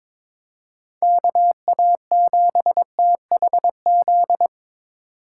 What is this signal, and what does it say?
Morse code, heard as a keyed tone, about 26 wpm, 700 Hz. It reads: KA7THZ
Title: Speed Racing - USA Call Signs 22wpm